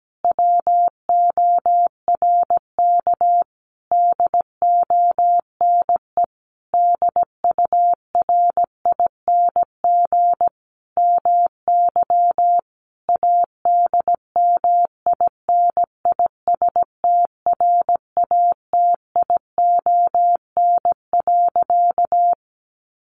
Morse code, heard as a keyed tone, about 17 wpm, 700 Hz. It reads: WORK DONE DURING MY ADMINISTRATION.
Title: Sentences from Top 1300 Words 17wpm